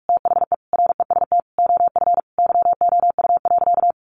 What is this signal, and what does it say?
Morse code, heard as a keyed tone, about 44 wpm, 700 Hz. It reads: THE REST OF YOU.
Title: Sentences from Top 700 Words 45wpm